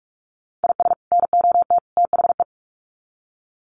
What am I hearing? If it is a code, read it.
Morse code, heard as a keyed tone, about 44 wpm, 700 Hz. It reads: IS NOT THE
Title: Sets of 3 Words 45wpm